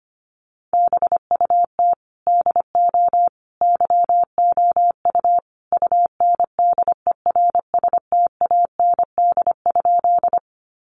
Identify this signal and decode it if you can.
Morse code, heard as a keyed tone, about 25 wpm, 700 Hz. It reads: BUT DO YOU UNDERSTAND?
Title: Sentences from Top 600 Words 25wpm